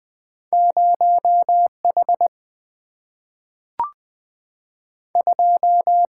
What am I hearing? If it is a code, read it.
Morse code, heard as a keyed tone, about 20 wpm, 690 Hz. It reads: 0 H 2
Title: Two Letter-Number Combinations - 20wpm